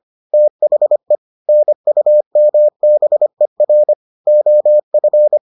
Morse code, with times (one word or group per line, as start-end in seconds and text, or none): 0.3-1.2 THE
1.5-3.9 NUMBER
4.3-5.4 OF